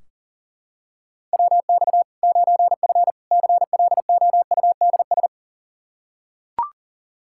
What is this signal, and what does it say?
Morse code, heard as a keyed tone, about 37 wpm, 700 Hz. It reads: WX 9F CLOUDS
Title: K1USN SST QSO Practice 40wpm